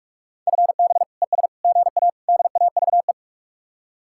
Morse code, heard as a keyed tone, about 45 wpm, 700 Hz. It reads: FB ES GA DAVE
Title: Speed-Racing Common QSO Phrases Encore - 45wpm